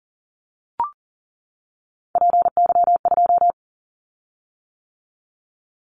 Morse code, heard as a keyed tone, about 40 wpm, 700 Hz. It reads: PY2
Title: Callsign Prefixes for Top 100 DXCC - 40wpm